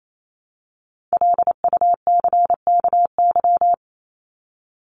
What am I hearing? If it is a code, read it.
Morse code, heard as a keyed tone, about 28 wpm, 700 Hz. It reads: LUCKY
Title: Top 1751-2000 Words 28wpm